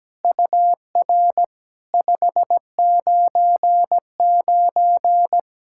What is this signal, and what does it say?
Morse code, heard as a keyed tone, about 17 wpm, 700 Hz. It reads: UR 599
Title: SOTA QSO Practice 17wpm